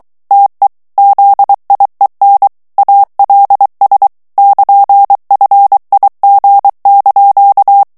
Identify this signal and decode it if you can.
Morse code, heard as a keyed tone, about 23 wpm, 800 Hz. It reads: TE ZIEN ALS <KN>FIG)
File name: HOLLANDAIS texte clair 1 vitesse 22 mots minute